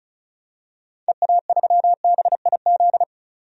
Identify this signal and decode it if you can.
Morse code, heard as a keyed tone, about 36 wpm, 700 Hz. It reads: EA3BIZ
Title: DXCC-100 Call Signs - 6 Characters - 35wpm